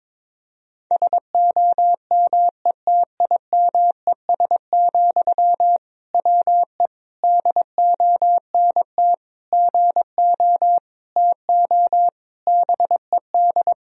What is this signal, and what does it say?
Morse code, heard as a keyed tone, about 21 wpm, 700 Hz. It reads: SOMETIMES, WE DONT GO TO BED
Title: Sentences from Top 1000 Words - Review Part 3 - 22wpm